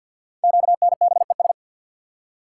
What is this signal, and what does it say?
Morse code, heard as a keyed tone, about 46 wpm, 700 Hz. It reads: QNBES
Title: Code Groups - 5 Letters - 50wpm